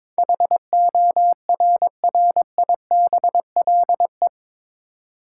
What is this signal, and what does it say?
Morse code, heard as a keyed tone, about 22 wpm, 700 Hz. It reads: HORRIBLE
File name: practice / 22wpm / Eight Letter Words - 22wpm